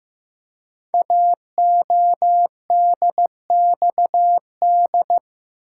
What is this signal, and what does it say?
Morse code, heard as a keyed tone, about 15 wpm, 700 Hz. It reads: AODXD
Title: Code Groups - 5 Letters - 15wpm